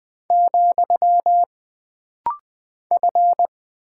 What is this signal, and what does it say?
Morse code, heard as a keyed tone, about 20 wpm, 700 Hz. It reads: , F
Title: Single Character (Letter:Number:Punctuation:Prosign) - Rapid-Fire - 20wpm